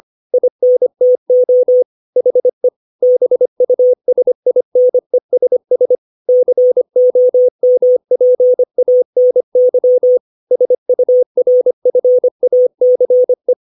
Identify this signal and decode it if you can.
Morse code, heard as a keyed tone, about 25 wpm, 500 Hz. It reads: INTO HE BUSINESS COMPANY SURFACE